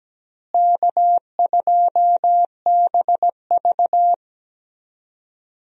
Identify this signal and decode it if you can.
Morse code, heard as a keyed tone, about 17 wpm, 700 Hz. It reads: K2BV
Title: USA Call Signs - 4 Characters - Repeated 2x - 17wpm